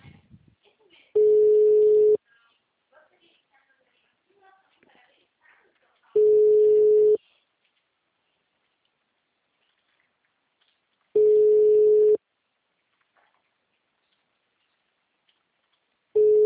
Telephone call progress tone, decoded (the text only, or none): ringback tone